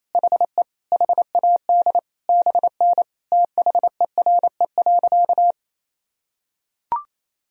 Morse code, heard as a keyed tone, about 28 wpm, 700 Hz. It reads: HE HAD BN THERE.
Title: Sentences from Top 300 Words and Abbreviations 28wpm